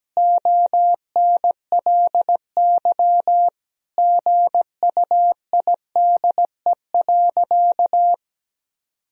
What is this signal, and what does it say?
Morse code, heard as a keyed tone, about 17 wpm, 700 Hz. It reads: ONLY GUIDE.
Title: Sentences from Top 1000 Words - Review Part 3 - 17wpm